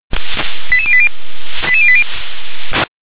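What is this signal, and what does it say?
Five-tone selective call (ZVEI), heard as quick runs of five tones, tones of about 70 ms. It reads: 88080 88080
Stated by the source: ZVEI five-tone selective call received on an SDR.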